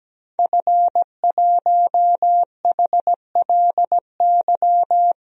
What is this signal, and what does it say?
Morse code, heard as a keyed tone, about 17 wpm, 700 Hz. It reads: F1HLY
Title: DXCC-100 Call Signs - 5 Characters - Repeated 3x - 17wpm